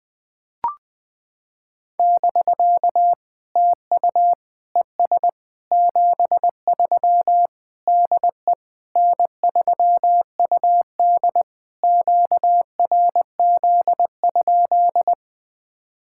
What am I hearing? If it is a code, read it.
Morse code, heard as a keyed tone, about 21 wpm, 700 Hz. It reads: <BK> TU ES 73 DE N3UD QRZ?
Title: POTA QSO Practice - 20wpm